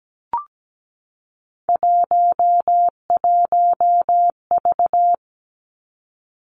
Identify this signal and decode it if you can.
Morse code, heard as a keyed tone, about 17 wpm, 700 Hz. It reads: 11V
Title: Three Challenging Letter-Number Combinations 17wpm